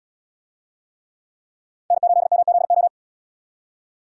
Morse code, heard as a keyed tone, about 77 wpm, 700 Hz. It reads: N8MZO